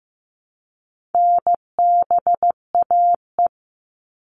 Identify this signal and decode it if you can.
Morse code, heard as a keyed tone, about 15 wpm, 700 Hz. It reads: NBAE
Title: All Cut Numbers - 4-Digit 15wpm